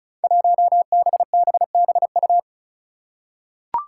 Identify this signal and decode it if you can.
Morse code, heard as a keyed tone, about 34 wpm, 700 Hz. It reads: 1BBBU
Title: Five Challenging Letter-Number Combinations 35wpm